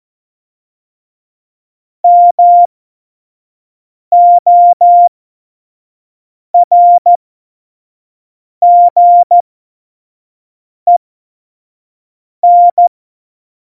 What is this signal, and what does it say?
Morse code, heard as a keyed tone, about 14 wpm, spaced as for 5 wpm, 700 Hz. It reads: MORGEN
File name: Morsekurs / Witze / Witz 20 March 2018, 25 F70